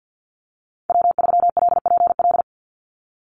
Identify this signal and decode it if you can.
Morse code, heard as a keyed tone, about 49 wpm, 700 Hz. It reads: W3LPL
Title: Call Signs Repeated 3x - 50wpm